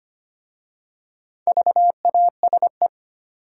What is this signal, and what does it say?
Morse code, heard as a keyed tone, about 25 wpm, 700 Hz. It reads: VASE
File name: practice / 25wpm / Easy Words with Hints - 25wpm